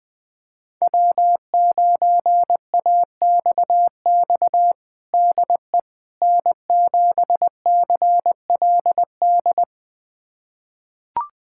Morse code, heard as a keyed tone, about 21 wpm, 700 Hz. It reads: W9AXX DE N7CLD
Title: USA Callsign DE Callsign - 5 Characters - 20wpm